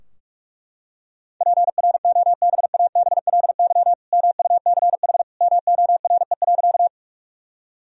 Morse code, heard as a keyed tone, about 42 wpm, 700 Hz. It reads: PROBABLY MUCH MORE.
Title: Sentences from Top 1000 Words - Review Part 4 - 45wpm